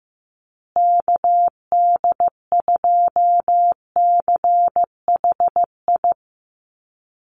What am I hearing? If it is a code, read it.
Morse code, heard as a keyed tone, about 15 wpm, 700 Hz. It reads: KD2CHI
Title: USA Call Signs Repeated 2x 15wpm